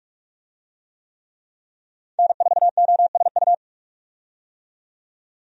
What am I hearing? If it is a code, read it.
Morse code, heard as a keyed tone, about 42 wpm, 700 Hz. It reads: N4OSU
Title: USA Call Signs - 5 Characters - 45wpm